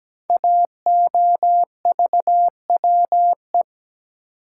Morse code, heard as a keyed tone, about 17 wpm, 700 Hz. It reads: AOVWE
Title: Code Groups - 5 Letters - 17wpm